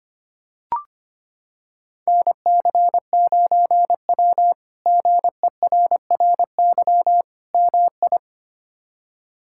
Morse code, heard as a keyed tone, about 25 wpm, 700 Hz. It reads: NC9W GERRY MI
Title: K1USN SST QSO Practice - 25wpm